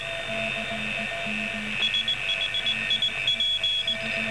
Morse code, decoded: SHIP